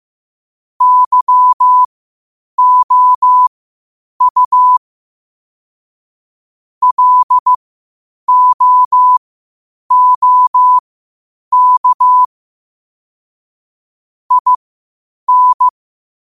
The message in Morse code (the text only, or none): YOU LOOK IN